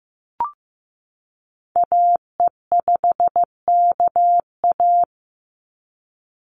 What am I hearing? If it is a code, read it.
Morse code, heard as a keyed tone, about 15 wpm, 700 Hz. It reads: AE5KA
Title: NA SOTA Activator Callsigns 15wpm